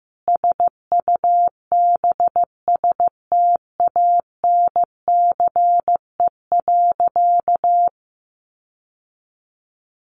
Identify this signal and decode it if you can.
Morse code, heard as a keyed tone, about 14 wpm, 700 Hz. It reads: SUBSTANCE.
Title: Sentences from Top 1500 Words 15wpm